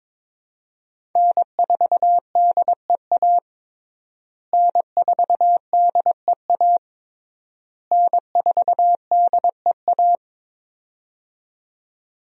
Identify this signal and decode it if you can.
Morse code, heard as a keyed tone, about 22 wpm, 700 Hz. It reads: N4DEA N4DEA N4DEA
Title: USA Call Signs - 5 Characters - Repeated 3x - 22wpm